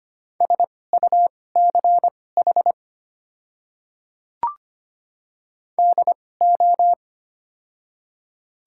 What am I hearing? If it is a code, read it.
Morse code, heard as a keyed tone, about 25 wpm, spaced as for 18 wpm, 700 Hz. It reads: SUCH DO
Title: 2x Character Spacing - Top 200 Words - 25wpm